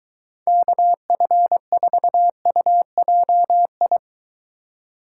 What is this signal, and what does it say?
Morse code, heard as a keyed tone, about 22 wpm, 700 Hz. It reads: KF4UJI